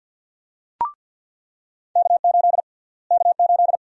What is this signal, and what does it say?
Morse code, heard as a keyed tone, about 49 wpm, 700 Hz. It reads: K8 K8
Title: USA Call Sign Prefixes - Repeated 2x - 50wpm